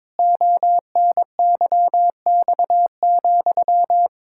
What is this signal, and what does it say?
Morse code, heard as a keyed tone, about 22 wpm, 700 Hz. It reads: ONYX,
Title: Pangram Challenge v2 - 22wpm